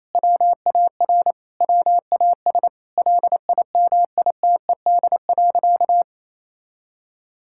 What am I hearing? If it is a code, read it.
Morse code, heard as a keyed tone, about 28 wpm, 700 Hz. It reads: WAR WAS LIMITED.
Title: Sentences from Top 1000 Words - Review Part 6 - 28wpm